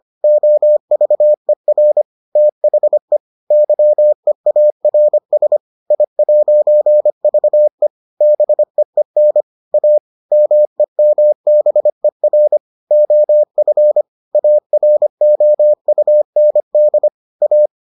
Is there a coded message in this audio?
Yes: Morse code from 0.2 to 17.7 s, heard as a keyed tone, about 25 wpm, 600 Hz. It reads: OVER THE YEARS I'VE BEEN A MEMBER OF AROUND A